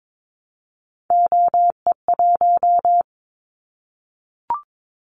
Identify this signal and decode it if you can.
Morse code, heard as a keyed tone, about 22 wpm, 700 Hz. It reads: OE1
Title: Callsign Prefixes for Top 25 DXCC - 22wpm